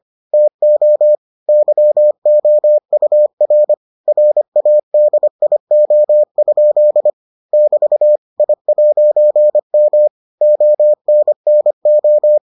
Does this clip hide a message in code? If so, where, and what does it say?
Yes: Morse code from 0.3 to 12.4 s, heard as a keyed tone, about 25 wpm, 600 Hz. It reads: TO YOUR RADIO? <BT> I'M ONNO